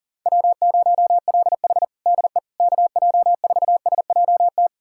Morse code, heard as a keyed tone, about 40 wpm, 700 Hz. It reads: W0LH DE KJ4SJT
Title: USA Callsign DE Callsign 40wpm